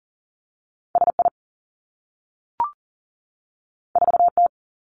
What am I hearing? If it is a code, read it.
Morse code, heard as a keyed tone, about 38 wpm, 700 Hz. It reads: S I 4 T
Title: Two Challenging Letter-Number Combinations 40wpm